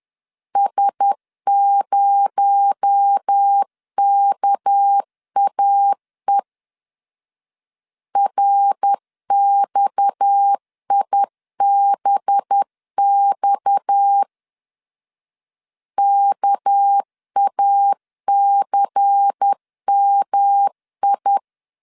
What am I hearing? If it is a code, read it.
Morse code, heard as a keyed tone, about 10 wpm, 800 Hz. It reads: S0KAE RXIBX KACMI
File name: CW / DET / 1 6 1500C S B 10wpm 4137